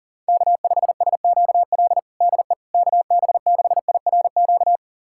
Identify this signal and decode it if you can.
Morse code, heard as a keyed tone, about 40 wpm, 700 Hz. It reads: K5SQL DE KB6IRQ
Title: USA Callsign DE Callsign 40wpm